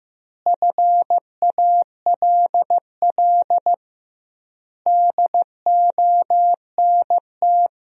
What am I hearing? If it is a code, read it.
Morse code, heard as a keyed tone, about 15 wpm, 700 Hz. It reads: FALL DONT
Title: 2x Word-Spacing - Sentences from Top 500 Words - 15wpm